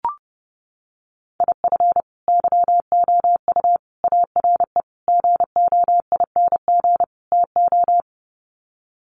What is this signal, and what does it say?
Morse code, heard as a keyed tone, about 30 wpm, 700 Hz. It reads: IF YOU ARE GOING TO
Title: Sets of 5 Words Encore2 30wpm